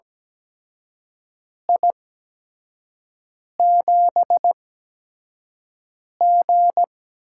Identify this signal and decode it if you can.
Morse code, heard as a keyed tone, about 17 wpm, spaced as for 5 wpm, 700 Hz. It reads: I7G
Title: USA Call Signs - 4 Characters - 8x Character Spacing - 17wpm